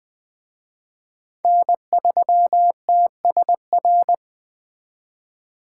Morse code, heard as a keyed tone, about 20 wpm, 700 Hz. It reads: N3TSR
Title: Speed Racing - USA Call Signs 17wpm